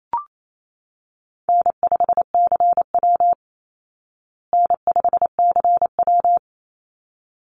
Morse code, heard as a keyed tone, about 28 wpm, 700 Hz. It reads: N5CW N5CW
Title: Easy Call Signs Repeated 2x - 28wpm